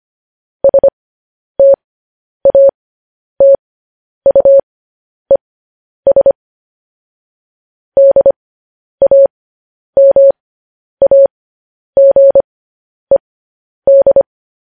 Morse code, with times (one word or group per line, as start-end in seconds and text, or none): 0.6-6.3 STATUES
8.0-14.2 DAMAGED